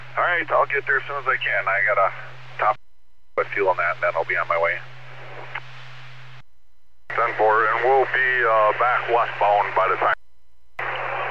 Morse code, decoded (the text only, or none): none